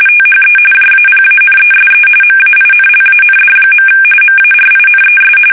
RTTY, about 45.45 baud. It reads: THE JAPANESE MAINLAND. MARINES
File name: fsk850